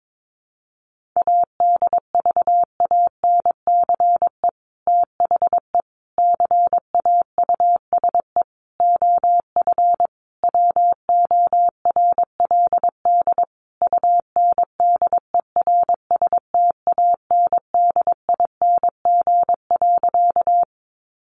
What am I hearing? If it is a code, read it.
Morse code, heard as a keyed tone, about 22 wpm, 700 Hz. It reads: ADVANCE THE CAUSE OF WORLD UNDERSTANDING.
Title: Sentences from Top 1400 Words 22wpm